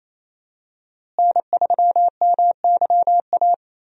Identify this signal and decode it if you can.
Morse code, heard as a keyed tone, about 28 wpm, 700 Hz. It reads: N3MYA
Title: USA Call Signs - 5 Characters - Repeated 2x - 28wpm